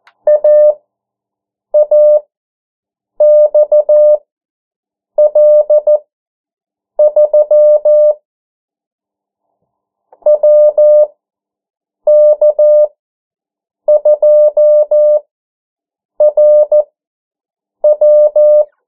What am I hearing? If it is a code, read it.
Morse code, heard as a keyed tone, about 14 wpm, spaced as for 7 wpm, 600 Hz. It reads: AAXL3 WK2RW
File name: morse mp3 / 09 urok 8